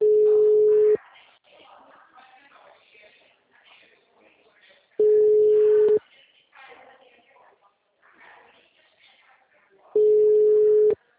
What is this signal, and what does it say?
Ringback tone (Europe, 425 Hz), with speech.